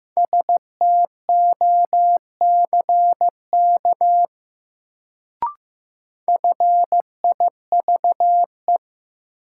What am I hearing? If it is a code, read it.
Morse code, heard as a keyed tone, about 15 wpm, 700 Hz. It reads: STOCK FIVE
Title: Sets of 5 Words - New Word Drill - 15wpm